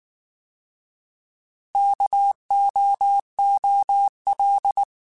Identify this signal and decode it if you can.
Morse code, heard as a keyed tone, about 19 wpm, 800 Hz. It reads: KOOL